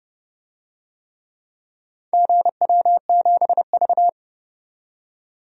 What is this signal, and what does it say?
Morse code, heard as a keyed tone, about 30 wpm, 700 Hz. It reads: GW7V
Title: DXCC-100 Call Signs - 4 Characters - 30wpm